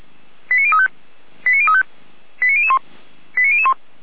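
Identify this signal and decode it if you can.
Five-tone selective call (ZVEI), heard as quick runs of five tones, tones of about 80 ms. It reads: 89035 89035 89001 89001
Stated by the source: ZVEI five-tone selective call received by SDR.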